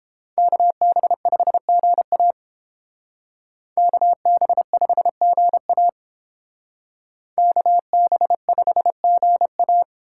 Morse code, heard as a keyed tone, about 29 wpm, 700 Hz. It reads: KB5GA KB5GA KB5GA